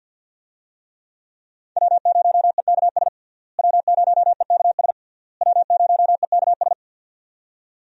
Morse code, heard as a keyed tone, about 51 wpm, 700 Hz. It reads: W0EKS W0EKS W0EKS
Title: USA Call Signs - 5 Characters - Repeated 3x - 50wpm